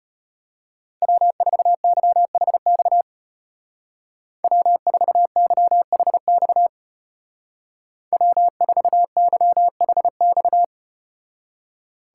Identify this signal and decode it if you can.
Morse code, heard as a keyed tone, about 32 wpm, 700 Hz. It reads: W4YHX W4YHX W4YHX